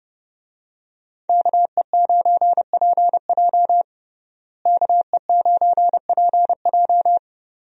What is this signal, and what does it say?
Morse code, heard as a keyed tone, about 30 wpm, 700 Hz. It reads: KE9PJ KE9PJ
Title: USA Call Signs - 5 Characters - Repeated 2x - 30wpm